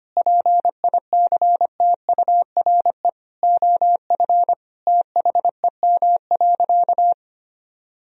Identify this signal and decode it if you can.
Morse code, heard as a keyed tone, about 25 wpm, 700 Hz. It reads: PICTURE OF THEM.
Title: Sentences from Top 1000 Words - Review Part 1 - 25wpm